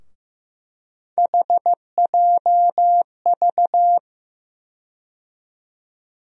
Morse code, heard as a keyed tone, about 15 wpm, 700 Hz. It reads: HJV
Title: USA Call Sign Postfixes - 3 Letters 15wpm